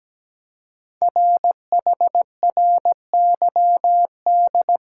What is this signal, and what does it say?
Morse code, heard as a keyed tone, about 17 wpm, 700 Hz. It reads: RHRYD
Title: Code Groups - 5 Letters - 17wpm